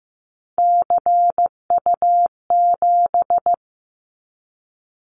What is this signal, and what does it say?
Morse code, heard as a keyed tone, about 15 wpm, 700 Hz. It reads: CU7
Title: Callsign Prefixes for Top 50 DXCC - 15wpm